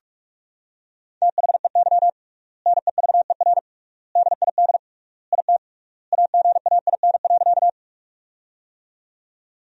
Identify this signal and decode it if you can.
Morse code, heard as a keyed tone, about 46 wpm, 700 Hz. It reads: THEY NEVER DID IT AGAIN.
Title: 3x Word-Spacing - Sentences from Top 300 Words - 45wpm